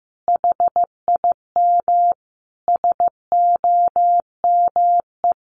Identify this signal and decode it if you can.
Morse code, heard as a keyed tone, about 15 wpm, 700 Hz. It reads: HIM SOME
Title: Sentences from Top 300 Words 15wpm